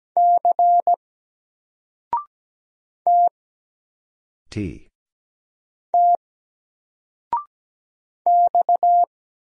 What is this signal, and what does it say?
Morse code, heard as a keyed tone, about 17 wpm, 700 Hz. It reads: C T T X
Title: Single Character (Letter:Number:Punctuation:Prosign) 17wpm